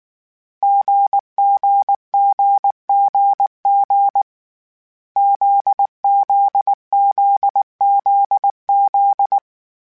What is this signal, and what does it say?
Morse code, heard as a keyed tone, about 19 wpm, 800 Hz. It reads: GGGGG ZZZZZ
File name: Difficult Combos / Difficult Combos-mp3s / GZ GZ7 @18